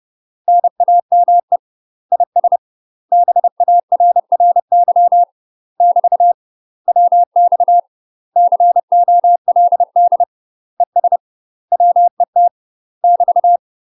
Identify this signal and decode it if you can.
Morse code, heard as a keyed tone, about 30 wpm, 700 Hz. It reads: NAME IS BARRY <BT> WX COLD ES WET <BT>
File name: UK MASTER QSOs TRAINING / 30WPM / UK MASTER QSOs 0029